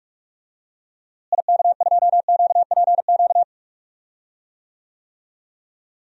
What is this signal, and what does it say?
Morse code, heard as a keyed tone, about 46 wpm, 700 Hz. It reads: IK2QPQ
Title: DXCC-100 Call Signs - 6 Characters - Repeated 2x - 45wpm